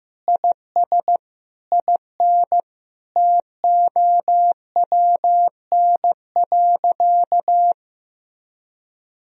Morse code, heard as a keyed tone, about 15 wpm, 700 Hz. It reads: IS IN TOWN.
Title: Sentences from Top 1000 Words - Review Part 4 - 15wpm